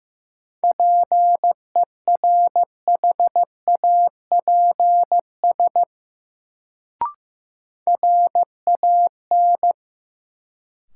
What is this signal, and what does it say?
Morse code, heard as a keyed tone, about 15 wpm, 700 Hz. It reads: PERHAPS RAN
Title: Sentences from Top 500 Words - New Word Drill - 15wpm